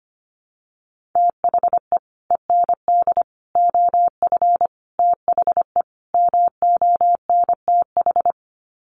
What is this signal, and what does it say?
Morse code, heard as a keyed tone, about 25 wpm, 700 Hz. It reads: THE END OF THE MONTH
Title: Sets of 5 Words Encore 25wpm